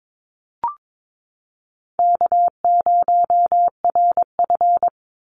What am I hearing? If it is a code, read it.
Morse code, heard as a keyed tone, about 22 wpm, 700 Hz. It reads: K0RF
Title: Easy Call Signs Repeated 3x - 22wpm